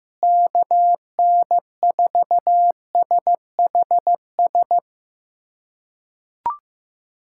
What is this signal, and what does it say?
Morse code, heard as a keyed tone, about 16 wpm, 700 Hz. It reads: KN4SHS
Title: USA Callsign DE Callsign 15wpm